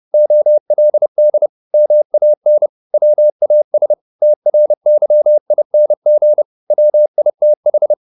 Morse code, heard as a keyed tone, about 30 wpm, 600 Hz. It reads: OLD MAN WAS TRYING WITH
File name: the old man and the sea / the old man and the sea 1 30wpm